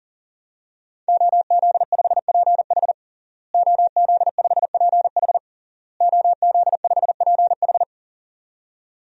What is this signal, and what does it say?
Morse code, heard as a keyed tone, about 40 wpm, 700 Hz. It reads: OZ5PH OZ5PH OZ5PH
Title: DXCC-100 Call Signs - 5 Characters - Repeated 3x - 40wpm